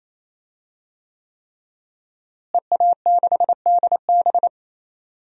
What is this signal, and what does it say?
Morse code, heard as a keyed tone, about 28 wpm, 700 Hz. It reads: EA6DB
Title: DXCC-100 Call Signs - 5 Characters - Repeated 3x - 28wpm